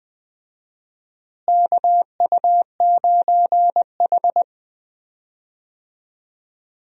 Morse code, heard as a keyed tone, about 21 wpm, 700 Hz. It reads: KU9H
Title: USA Call Signs - 4 Characters - Repeated 2x - 20wpm